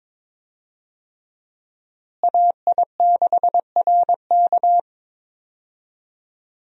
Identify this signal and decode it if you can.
Morse code, heard as a keyed tone, about 21 wpm, 700 Hz. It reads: AI6RK
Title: USA Call Signs - 5 Characters - Repeated 3x - 22wpm